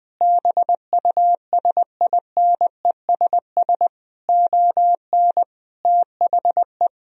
Morse code, heard as a keyed tone, about 20 wpm, 700 Hz. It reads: BUSINESS ON THE
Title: Sentences from Top 1000 Words - Review Part 4 - 20wpm